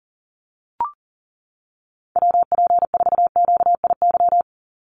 Morse code, heard as a keyed tone, about 40 wpm, 700 Hz. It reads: WP4QIY
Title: USA Call Signs 40wpm